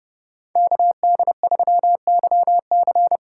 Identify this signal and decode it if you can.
Morse code, heard as a keyed tone, about 29 wpm, 700 Hz. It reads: KD3YC